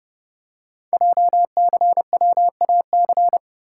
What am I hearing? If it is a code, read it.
Morse code, heard as a keyed tone, about 30 wpm, 700 Hz. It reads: JCWAC
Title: Code Groups - 5 Letters - 30wpm